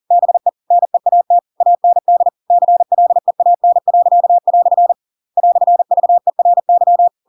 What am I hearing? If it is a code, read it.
Morse code, heard as a keyed tone, about 40 wpm, 700 Hz. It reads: BE NEAT AND CLEAN." "VERY
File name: Pinocchio / Pinocchio ch8 40 40 ik1hge